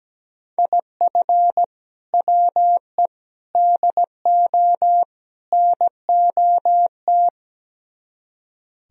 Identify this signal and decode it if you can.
Morse code, heard as a keyed tone, about 17 wpm, 700 Hz. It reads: IF WE DO NOT
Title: Speed-Racing - Sets of 4 Words 17wpm